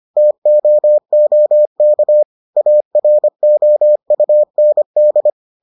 Morse code, heard as a keyed tone, about 25 wpm, 600 Hz. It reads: TOOK AROUND